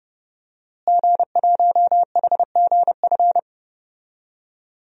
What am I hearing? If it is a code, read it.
Morse code, heard as a keyed tone, about 31 wpm, 700 Hz. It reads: G1HGF
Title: DXCC-100 Call Signs - 5 Characters - Repeated 2x - 30wpm